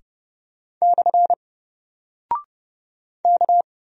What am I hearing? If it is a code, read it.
Morse code, heard as a keyed tone, about 29 wpm, 700 Hz. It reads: / K
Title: Single Character (Letter:Number:Punctuation:Prosign) 30wpm